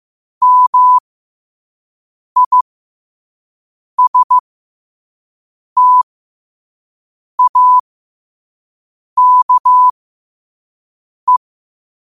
Morse code, heard as a keyed tone, about 15 wpm, spaced as for 5 wpm, 1000 Hz. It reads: MISTAKE
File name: 5wpm